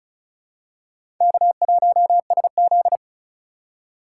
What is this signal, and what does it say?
Morse code, heard as a keyed tone, about 36 wpm, 700 Hz. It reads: K1SZ
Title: USA Call Signs - 2x Character Spacing - 35wpm